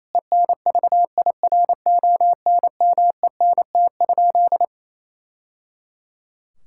Morse code, heard as a keyed tone, about 28 wpm, 700 Hz. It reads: ENVIRONMENT?
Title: Sentences from Top 1000 Words - Review Part 5 - 28wpm